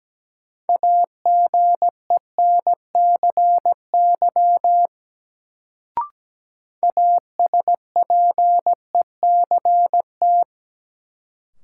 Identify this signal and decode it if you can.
Morse code, heard as a keyed tone, about 17 wpm, 700 Hz. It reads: AGENCY ASPECT
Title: Sentences from Top 1300 Words - New Word Drill - 17wpm